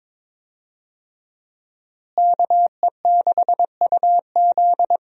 Morse code, heard as a keyed tone, about 22 wpm, 700 Hz. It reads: KE6UZ
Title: USA Call Signs - 5 Characters - Repeated 3x - 22wpm